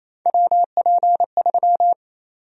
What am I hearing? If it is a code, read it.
Morse code, heard as a keyed tone, about 27 wpm, 700 Hz. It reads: WP3
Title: USA Call Sign Prefixes - 28wpm